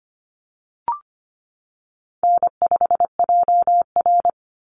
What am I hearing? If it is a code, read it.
Morse code, heard as a keyed tone, about 25 wpm, 700 Hz. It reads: N5JR
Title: Easy Call Signs Repeated 2x - 25wpm